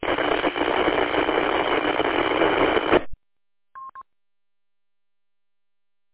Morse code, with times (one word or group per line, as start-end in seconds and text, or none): none